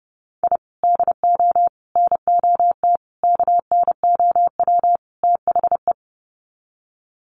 Morse code, heard as a keyed tone, about 30 wpm, 700 Hz. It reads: I DO NOT KNOW THE
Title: Sets of 5 Words 30wpm